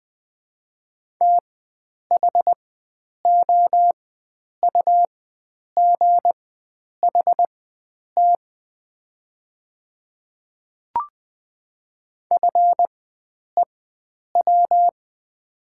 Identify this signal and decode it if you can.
Morse code, heard as a keyed tone, about 20 wpm, spaced as for 9 wpm, 700 Hz. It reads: THOUGHT FEW
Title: 4x Character Spacing - Top 500 Words - 20wpm